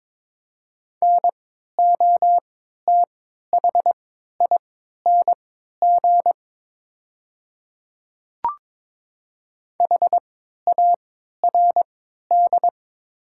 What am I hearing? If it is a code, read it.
Morse code, heard as a keyed tone, about 22 wpm, spaced as for 12 wpm, 700 Hz. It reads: NOTHING HARD
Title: 3x Character Spacing - Top 500 Words - 22wpm